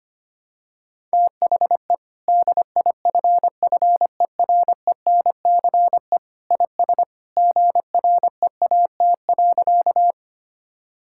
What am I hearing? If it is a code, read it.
Morse code, heard as a keyed tone, about 25 wpm, 700 Hz. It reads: THE DIFFERENCE IS GREAT.
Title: Sentences from Top 1000 Words - Review Part 5 - 25wpm